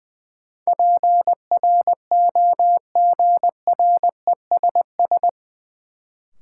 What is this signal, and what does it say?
Morse code, heard as a keyed tone, about 20 wpm, 700 Hz. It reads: PROGRESS
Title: Sentences from Top 1400 Words - New Word Drill Alt - 20wpm